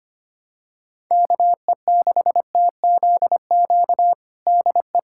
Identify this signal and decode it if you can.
Morse code, heard as a keyed tone, about 25 wpm, 700 Hz. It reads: KE6TZQ DE
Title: USA Callsign DE Callsign 25wpm